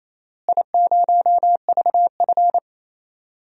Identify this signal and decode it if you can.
Morse code, heard as a keyed tone, about 28 wpm, 700 Hz. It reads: I0VF
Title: DXCC-100 Call Signs - 4 Characters - Repeated 3x - 28wpm